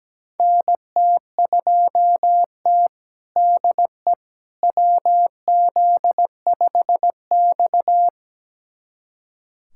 Morse code, heard as a keyed tone, about 17 wpm, 700 Hz. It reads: NT2T DE WZ5X
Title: USA Callsign DE Callsign - 4 Characters - 17wpm